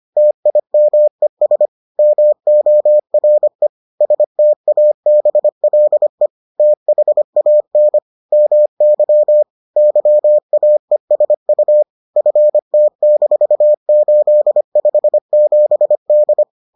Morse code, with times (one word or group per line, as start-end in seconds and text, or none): 0.2-1.7 TIMES
2.0-3.7 MORE
4.0-6.3 STABLE
6.6-8.0 THAN
8.3-9.4 MY
9.8-11.8 YAESU
12.2-16.4 FT-857D